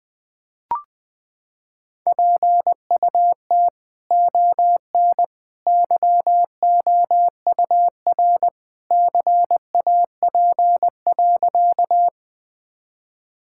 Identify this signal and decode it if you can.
Morse code, heard as a keyed tone, about 20 wpm, 700 Hz. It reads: PUT ON YOUR CAP.
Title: Sentences from Top 1600 Words 20wpm